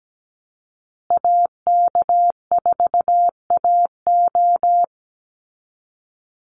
Morse code, heard as a keyed tone, about 17 wpm, 700 Hz. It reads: AK4AO
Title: Call Signs Repeated 3x - 17wpm